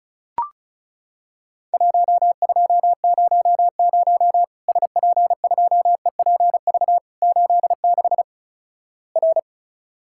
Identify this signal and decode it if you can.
Morse code, heard as a keyed tone, about 36 wpm, 700 Hz. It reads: 1200 SP2EPV 86 R
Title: Worked All Europe DX Contest - QTC Exchange - 35wpm